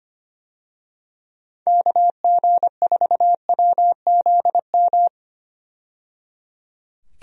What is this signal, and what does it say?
Morse code, heard as a keyed tone, about 25 wpm, 700 Hz. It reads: KG4WZM
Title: USA Call Signs - 6 Characters - 25wpm